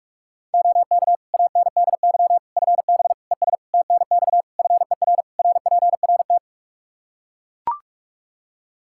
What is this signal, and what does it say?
Morse code, heard as a keyed tone, about 41 wpm, 700 Hz. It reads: OK ANDY FB ES TNX FER RPRT
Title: Practice QSO - Exchange 2 - Info - 45wpm